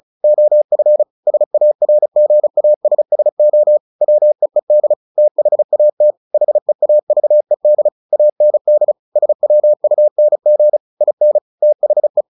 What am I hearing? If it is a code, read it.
Morse code, heard as a keyed tone, about 34 wpm, 600 Hz. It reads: OF SARGASSO WEED THAT HEAVED AND SWUNG IN THE
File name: the old man and the sea 2 35wpm